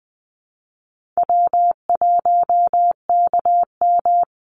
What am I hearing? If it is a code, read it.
Morse code, heard as a keyed tone, about 20 wpm, 700 Hz. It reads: W1KM
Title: Easy Call Signs - 20wpm